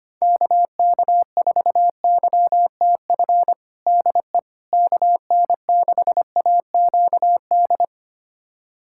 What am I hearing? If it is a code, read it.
Morse code, heard as a keyed tone, about 25 wpm, 700 Hz. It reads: KK4YTF DE KN6AQD
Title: USA Callsign DE Callsign - 6 Characters - 25wpm